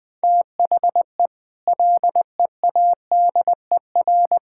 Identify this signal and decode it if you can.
Morse code, heard as a keyed tone, about 20 wpm, 700 Hz. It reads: THE LEADER
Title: Sets of 6 Words - 20wpm